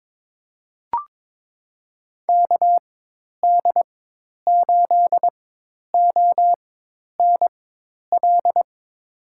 Morse code, heard as a keyed tone, about 22 wpm, spaced as for 10 wpm, 700 Hz. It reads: KD8ONL
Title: USA Call Signs - 6 Characters - 4x Character Spacing - 22wpm